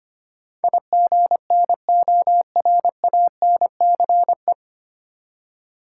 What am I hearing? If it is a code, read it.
Morse code, heard as a keyed tone, about 25 wpm, 700 Hz. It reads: IGNORANCE
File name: practice / 25wpm / 9 Letter Words - First 3 Letters Given - 25wpm